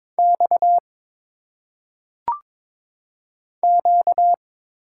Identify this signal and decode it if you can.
Morse code, heard as a keyed tone, about 22 wpm, 700 Hz. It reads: X Q
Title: Single Letters - Phonetic Pronunciation 22wpm